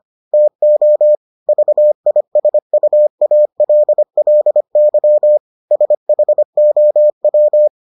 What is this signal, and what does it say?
Morse code, heard as a keyed tone, about 25 wpm, 600 Hz. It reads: TO VISUALLY SHOW